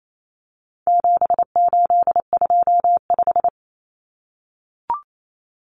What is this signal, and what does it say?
Morse code, heard as a keyed tone, about 28 wpm, 700 Hz. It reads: 7825
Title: Numbers - 4-Digit 28wpm